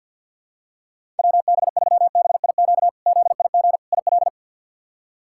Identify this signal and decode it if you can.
Morse code, heard as a keyed tone, about 48 wpm, 700 Hz. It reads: WB3BIQ ZIG IL
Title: K1USN SST QSO Practice - 50wpm